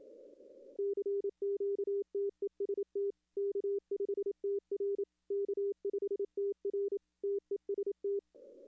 Morse code, heard as a keyed tone, about 27 wpm, 400 Hz. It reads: CQTEST K5TR K5TR TEST
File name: contests / arrl-160-k5tr-op-w5tn -2300